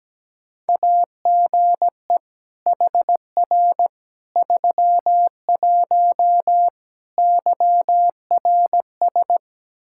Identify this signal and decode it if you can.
Morse code, heard as a keyed tone, about 17 wpm, 700 Hz. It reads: AGE HR 31 YRS
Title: Practice QSO - Exchange 3 - More Info - 17wpm